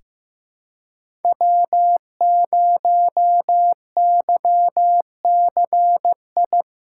Morse code, heard as a keyed tone, about 15 wpm, 700 Hz. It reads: W0YCI
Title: USA Callsign DE Callsign - 5 Characters - 15wpm